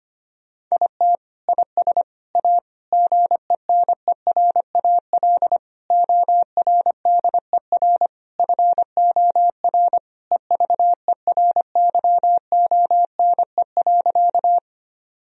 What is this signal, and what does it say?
Morse code, heard as a keyed tone, about 25 wpm, 700 Hz. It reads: IT IS A GENERAL ORDER FOR EVERYONE.
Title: Sentences from Top 1000 Words - Review Part 3 - 25wpm